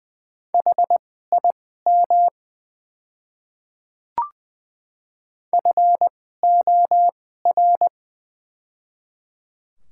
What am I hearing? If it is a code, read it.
Morse code, heard as a keyed tone, about 20 wpm, spaced as for 14 wpm, 700 Hz. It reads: HIM FOR
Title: 2x Character Spacing - Top 100 Words - 20wpm